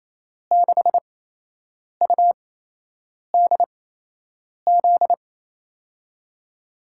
Morse code, heard as a keyed tone, about 28 wpm, spaced as for 8 wpm, 700 Hz. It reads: 6UDZ
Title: USA Call Signs - 6 Characters - 8x Character Spacing - 28wpm